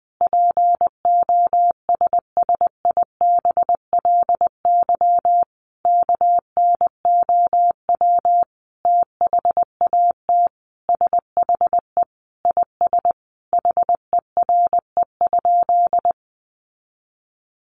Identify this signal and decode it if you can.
Morse code, heard as a keyed tone, about 20 wpm, 700 Hz. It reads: POSSIBLY KNOW THAT SHE IS HERE?
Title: Sentences from Top 1300 Words 20wpm